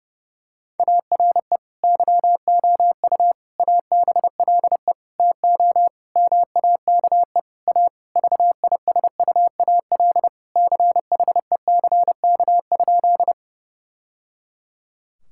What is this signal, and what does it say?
Morse code, heard as a keyed tone, about 32 wpm, 700 Hz. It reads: ARE YOU ABLE TO MAKE A VISUAL CHECK?
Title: Sentences from Top 1000 Words - Review Part 6 - 30wpm